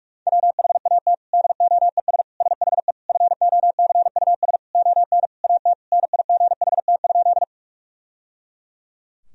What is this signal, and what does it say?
Morse code, heard as a keyed tone, about 46 wpm, 700 Hz. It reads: WHAT DOES SHE FOCUS ON AT NIGHT?
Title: Sentences from Top 1000 Words - Review Part 4 - 45wpm